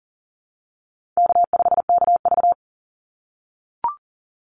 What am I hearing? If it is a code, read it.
Morse code, heard as a keyed tone, about 42 wpm, 700 Hz. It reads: K5KV
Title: Easy Call Signs - 40wpm